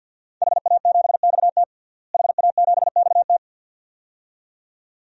Morse code, heard as a keyed tone, about 50 wpm, 700 Hz. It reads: HA7XT HA7XT
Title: DXCC-100 Call Signs - 5 Characters - Repeated 2x - 50wpm